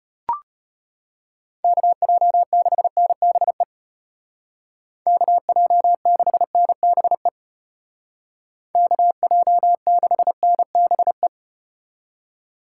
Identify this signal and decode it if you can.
Morse code, heard as a keyed tone, about 34 wpm, 700 Hz. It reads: KJ6NBE KJ6NBE KJ6NBE